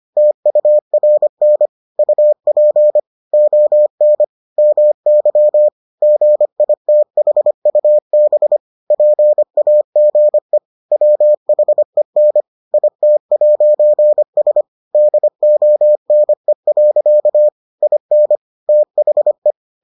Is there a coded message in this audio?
Yes: Morse code from 0.2 to 19.5 s, heard as a keyed tone, about 25 wpm, 600 Hz. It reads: TURN UP ON MY GITHUB PAGE WHEN IT'S DONE. IN THE